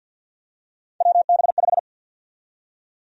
Morse code, heard as a keyed tone, about 52 wpm, 700 Hz. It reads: WB5
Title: USA Call Sign Prefixes - 50wpm